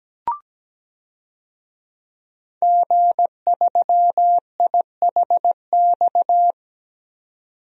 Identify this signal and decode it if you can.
Morse code, heard as a keyed tone, about 17 wpm, 700 Hz. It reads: G3IHX
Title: DXCC-100 Call Signs - 5 Characters - 17wpm